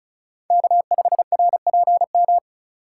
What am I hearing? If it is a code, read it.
Morse code, heard as a keyed tone, about 36 wpm, 700 Hz. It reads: K5RPM
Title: USA Call Signs - 3x Character Spacing - 35wpm